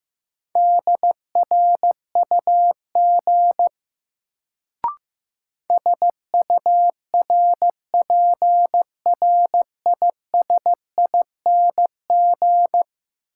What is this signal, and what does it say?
Morse code, heard as a keyed tone, about 15 wpm, 700 Hz. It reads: DRUG SURPRISING
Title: Sets of 5 Words - New Word Drill - 15wpm